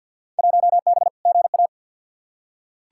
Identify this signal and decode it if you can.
Morse code, heard as a keyed tone, about 47 wpm, 700 Hz. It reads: 1B GA
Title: Field Day Exchanges - 50wpm